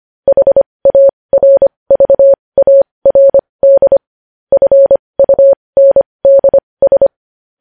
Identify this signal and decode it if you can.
Morse code, heard as a keyed tone, about 25 wpm, spaced as for 20 wpm, 550 Hz. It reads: HARVARD FUNDS
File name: podcast / News headlines-1x-25x20